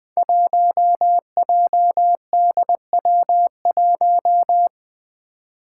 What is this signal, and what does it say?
Morse code, heard as a keyed tone, about 20 wpm, 700 Hz. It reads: 1JDW1
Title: Five Challenging Letter-Number Combinations 20wpm